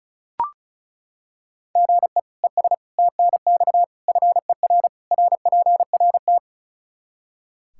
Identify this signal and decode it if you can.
Morse code, heard as a keyed tone, about 34 wpm, 700 Hz. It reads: GE ES TNX FER RPRT
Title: Practice QSO - Exchange 1 - Beginning - 35wpm